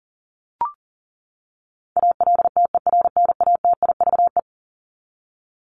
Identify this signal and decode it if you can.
Morse code, heard as a keyed tone, about 40 wpm, 700 Hz. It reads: ALTERNATIVE
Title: Top 501-750 Words 40wpm